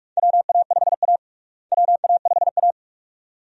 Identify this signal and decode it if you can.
Morse code, heard as a keyed tone, about 46 wpm, 700 Hz. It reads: WA5A WA5A
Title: USA Call Signs - 4 Characters - Repeated 2x - 45wpm